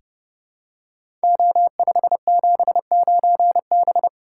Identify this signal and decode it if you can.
Morse code, heard as a keyed tone, about 28 wpm, 700 Hz. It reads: O579B
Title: Code Groups - 5 Letter:Numbers - 30wpm